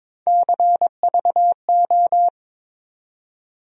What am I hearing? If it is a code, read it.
Morse code, heard as a keyed tone, about 21 wpm, 700 Hz. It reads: CVO
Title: USA Call Sign Postfixes - 3 Letters 22wpm